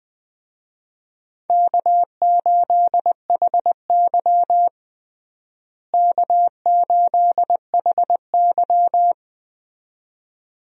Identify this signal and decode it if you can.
Morse code, heard as a keyed tone, about 20 wpm, 700 Hz. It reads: K8HY K8HY
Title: USA Call Signs - 4 Characters - Repeated 2x - 20wpm